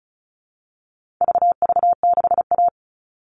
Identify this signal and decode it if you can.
Morse code, heard as a keyed tone, about 34 wpm, 700 Hz. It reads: VV6A
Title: All Cut Numbers - 4-Digit 35wpm